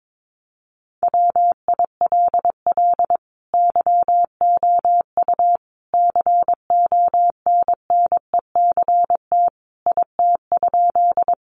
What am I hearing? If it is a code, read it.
Morse code, heard as a keyed tone, about 22 wpm, 700 Hz. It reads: WILL YOU CONNECT IT?
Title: Sentences from Top 1500 Words 22wpm